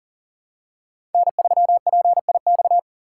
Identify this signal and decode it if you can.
Morse code, heard as a keyed tone, about 40 wpm, 700 Hz. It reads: N3PIX
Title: USA Call Signs - 5 Characters - 4x Character Spacing - 40wpm